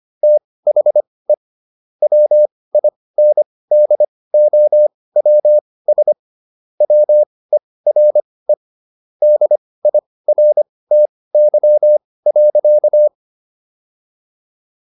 Morse code, heard as a keyed tone, about 25 wpm, spaced as for 18 wpm, 600 Hz. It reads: THE WINDOWS WERE DIRTY.